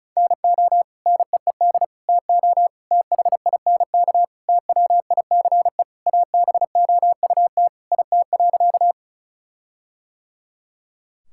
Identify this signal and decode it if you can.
Morse code, heard as a keyed tone, about 36 wpm, 700 Hz. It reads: NO NEED TO THINK TWICE ABOUT IT.
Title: Sentences from Top 1600 Words 35wpm